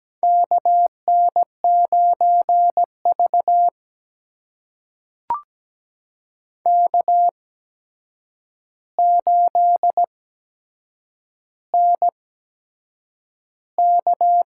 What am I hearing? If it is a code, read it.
Morse code, heard as a keyed tone, about 17 wpm, spaced as for 7 wpm, 700 Hz. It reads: KN9V K8NK
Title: USA Call Signs - 8x Character Spacing - 17wpm